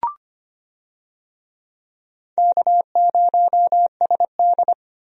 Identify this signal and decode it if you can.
Morse code, heard as a keyed tone, about 25 wpm, 700 Hz. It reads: K0SD
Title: USA Call Signs - 4 Characters - 25wpm